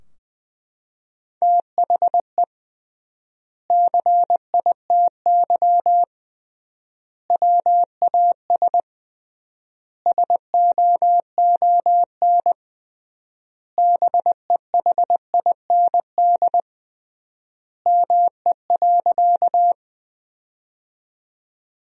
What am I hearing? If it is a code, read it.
Morse code, heard as a keyed tone, about 21 wpm, 700 Hz. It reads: THE CITY WAS SOON BEHIND ME.
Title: 3x Word-Spacing - Sentences from Top 500 Words - 20wpm